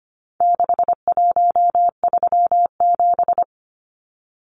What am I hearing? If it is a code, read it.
Morse code, heard as a keyed tone, about 25 wpm, 700 Hz. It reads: 6137
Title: Numbers - 4-Digit 25wpm